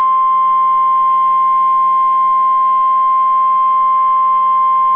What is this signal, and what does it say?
Emergency Alert System (SAME) tones (the 1050 Hz attention tone). It is heard from 0.4 to 4.7 s.